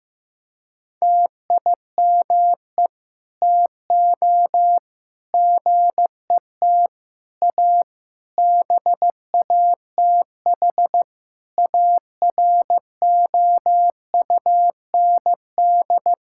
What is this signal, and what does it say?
Morse code, heard as a keyed tone, about 15 wpm, 700 Hz. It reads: TIME TO GET A BATH AROUND
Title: Sentences from Top 1600 Words 15wpm